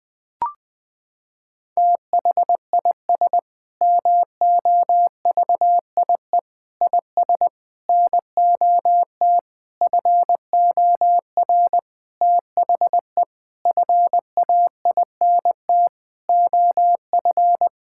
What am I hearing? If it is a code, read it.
Morse code, heard as a keyed tone, about 20 wpm, 700 Hz. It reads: THIS MOVIE IS NOT FOR THE FAINT OF
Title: Sentences with 6-Word Phrases - 20wpm